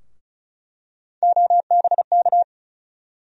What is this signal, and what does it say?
Morse code, heard as a keyed tone, about 36 wpm, 700 Hz. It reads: OBK
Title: USA Call Sign Postfixes - 3 Letters 35wpm